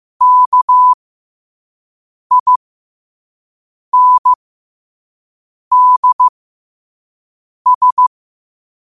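Morse code, heard as a keyed tone, about 15 wpm, spaced as for 5 wpm, 1000 Hz. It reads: KINDS
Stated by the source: Morse code practice at 5 wpm.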